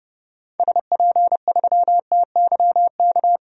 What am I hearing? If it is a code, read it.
Morse code, heard as a keyed tone, about 32 wpm, 700 Hz. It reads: SP3TYK
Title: DXCC-100 Call Signs - 6 Characters - 30wpm